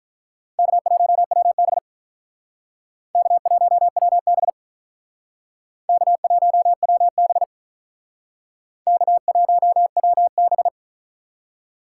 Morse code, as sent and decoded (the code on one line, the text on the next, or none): -.- .---- .-- -... / -.- .---- .-- -... / -.- .---- .-- -... / -.- .---- .-- -...
K1WB K1WB K1WB K1WB